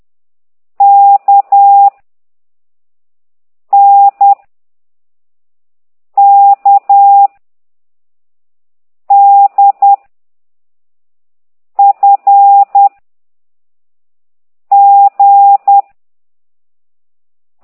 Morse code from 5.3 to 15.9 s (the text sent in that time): K D F G